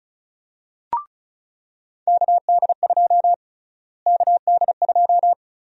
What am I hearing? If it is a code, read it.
Morse code, heard as a keyed tone, about 36 wpm, 700 Hz. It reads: KD2 KD2
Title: USA Call Sign Prefixes - Repeated 2x - 35wpm